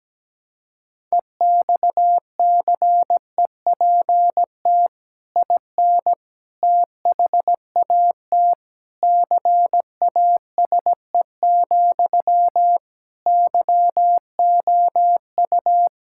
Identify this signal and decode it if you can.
Morse code, heard as a keyed tone, about 17 wpm, 700 Hz. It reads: EXCEPT IN THAT CASE, YOU
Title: Sentences from Top 1600 Words 17wpm